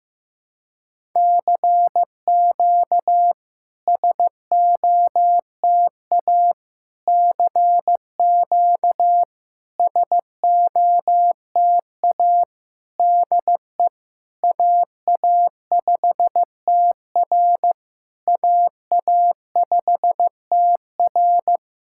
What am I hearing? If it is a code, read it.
Morse code, heard as a keyed tone, about 15 wpm, 700 Hz. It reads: CQ SOTA CQ SOTA DE AA5TR AA5TR
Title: SOTA QSO Practice 15wpm